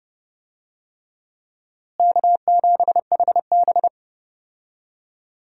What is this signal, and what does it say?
Morse code, heard as a keyed tone, about 30 wpm, 700 Hz. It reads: K7HB
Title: USA Call Signs - 4 Characters - Repeated 3x - 30wpm